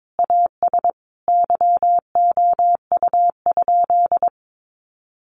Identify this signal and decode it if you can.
Morse code, heard as a keyed tone, about 22 wpm, 700 Hz. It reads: AS YOU?
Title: Sentences from Top 1000 Words 22wpm